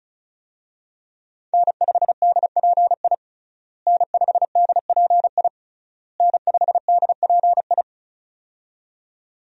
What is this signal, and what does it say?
Morse code, heard as a keyed tone, about 34 wpm, 700 Hz. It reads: N5DPI N5DPI N5DPI
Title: USA Call Signs - 5 Characters - Repeated 3x - 35wpm